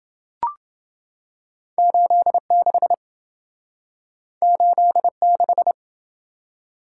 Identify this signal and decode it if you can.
Morse code, heard as a keyed tone, about 27 wpm, 700 Hz. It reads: 86 86